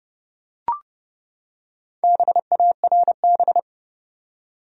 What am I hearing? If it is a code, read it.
Morse code, heard as a keyed tone, about 32 wpm, 700 Hz. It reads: BARB
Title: USA Common Names - Max 5 Letters - 30wpm